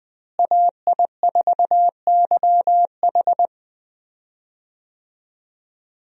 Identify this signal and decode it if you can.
Morse code, heard as a keyed tone, about 20 wpm, 700 Hz. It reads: AI4YH
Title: USA Call Signs - 5 Characters - Repeated 2x - 20wpm